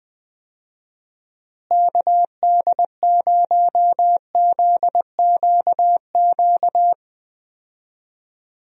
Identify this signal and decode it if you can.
Morse code, heard as a keyed tone, about 20 wpm, 700 Hz. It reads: KD0ZQQ
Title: USA Call Signs - 6 Characters - Repeated 2x - 20wpm